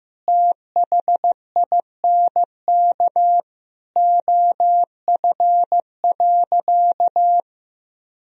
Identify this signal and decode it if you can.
Morse code, heard as a keyed tone, about 16 wpm, 700 Hz. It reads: THINK OF.
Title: Sentences from Top 1000 Words - Review Part 1 - 15wpm